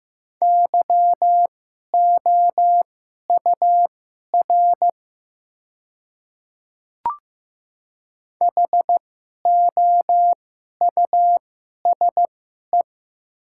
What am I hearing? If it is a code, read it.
Morse code, heard as a keyed tone, about 15 wpm, spaced as for 11 wpm, 700 Hz. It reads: YOUR HOUSE
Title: 2x Character Spacing - Top 200 Words - 15wpm